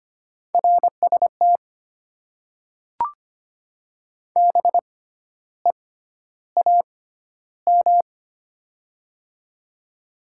Morse code, heard as a keyed tone, about 25 wpm, spaced as for 12 wpm, 690 Hz. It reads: RST BEAM
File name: Top 90 QSO Elements - 6x Character Spacing - 25wpm